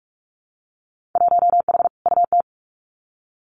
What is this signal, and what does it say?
Morse code, heard as a keyed tone, about 46 wpm, 700 Hz. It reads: 1H UT
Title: Winter Field Day Exchanges 45wpm